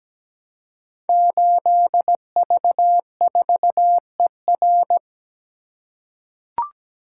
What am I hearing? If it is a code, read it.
Morse code, heard as a keyed tone, about 17 wpm, 700 Hz. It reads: 8V4ER
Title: Code Groups - 5 Letter:Numbers - 17wpm